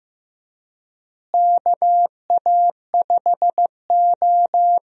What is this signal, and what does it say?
Morse code, heard as a keyed tone, about 15 wpm, 700 Hz. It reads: KA5O
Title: USA Call Signs - 4 Characters - Repeated 2x - 15wpm